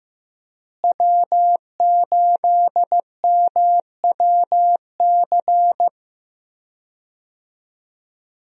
Morse code, heard as a keyed tone, about 16 wpm, 700 Hz. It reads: W8MWC
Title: USA Call Signs - 5 Characters - Repeated 2x - 15wpm